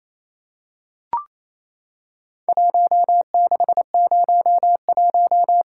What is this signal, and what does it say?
Morse code, heard as a keyed tone, about 28 wpm, 700 Hz. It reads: 1601
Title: Worked All Europe DX Contest - QTC Exchange - 28wpm